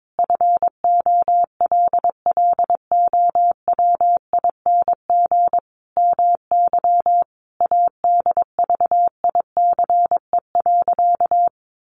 Morse code, heard as a keyed tone, about 22 wpm, 700 Hz. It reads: FOLLOWING MY ADVICE.
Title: Sentences from Top 1100 Words 22wpm